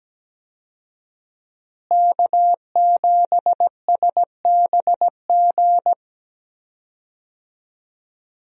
Morse code, heard as a keyed tone, about 17 wpm, 700 Hz. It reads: K7SBG
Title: USA Call Signs - 5 Characters - Repeated 2x - 17wpm